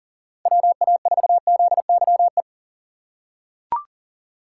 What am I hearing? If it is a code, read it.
Morse code, heard as a keyed tone, about 39 wpm, 700 Hz. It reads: WA4ZYE
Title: USA Call Signs - 3x Character Spacing - 40wpm